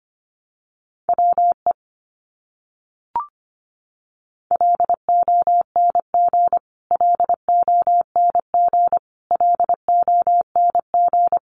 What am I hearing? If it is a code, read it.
Morse code, heard as a keyed tone, about 25 wpm, 700 Hz. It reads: WE LONG LONG LONG
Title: Top 200 Words Repeated 25wpm